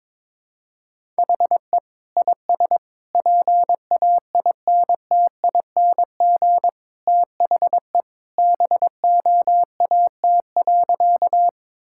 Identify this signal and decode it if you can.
Morse code, heard as a keyed tone, about 22 wpm, 700 Hz. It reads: HE IS PAINTING THE BOAT.
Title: Sentences from Top 1000 Words - Review Part 6 - 22wpm